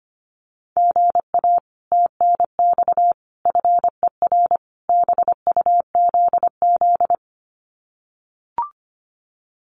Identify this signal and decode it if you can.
Morse code, heard as a keyed tone, about 25 wpm, 700 Hz. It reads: GA TNX FER BUZZ
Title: Common QSO Phrases Encore 25wpm